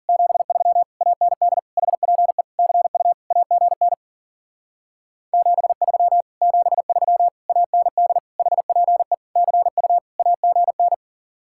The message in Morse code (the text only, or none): 73 AND HPE CU AGN 73 73 AND HPE CU AGN